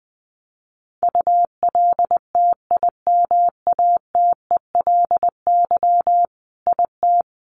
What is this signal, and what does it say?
Morse code, heard as a keyed tone, about 20 wpm, 700 Hz. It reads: ULTIMATELY IT
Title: Sentences from Top 1200 Words 20wpm